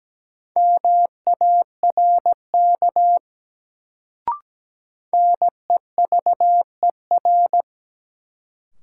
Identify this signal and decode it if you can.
Morse code, heard as a keyed tone, about 17 wpm, 700 Hz. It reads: MARK NEVER
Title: Sentences from Top 300 Words - New Word Drill - 17wpm